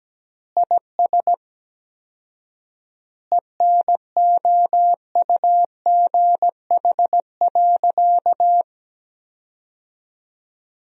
Morse code, heard as a keyed tone, about 17 wpm, 700 Hz. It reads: IS ENOUGH.
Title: 4x Word-Spacing - Sentences from Top 400 Words - 17wpm